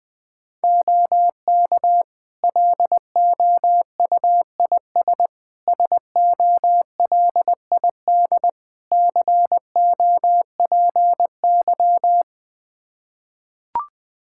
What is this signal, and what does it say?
Morse code, heard as a keyed tone, about 21 wpm, 700 Hz. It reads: OK LOUIS SOLID COPY
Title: Practice QSO - Exchange 3 - More Info - 20wpm